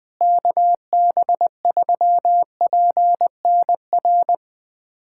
Morse code, heard as a keyed tone, about 20 wpm, 700 Hz. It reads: KB3PNR
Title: USA Callsign DE Callsign - 6 Characters - 20wpm